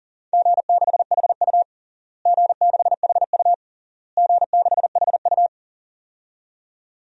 Morse code, heard as a keyed tone, about 40 wpm, 700 Hz. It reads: G6HU G6HU G6HU
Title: DXCC-100 Call Signs - 4 Characters - Repeated 3x - 40wpm